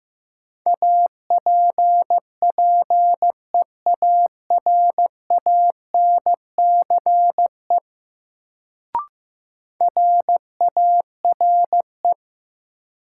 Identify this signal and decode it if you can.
Morse code, heard as a keyed tone, about 15 wpm, 700 Hz. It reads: APPEARANCE RARE
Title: Sentences from Top 1100 Words - New Word Drill Alt - 15wpm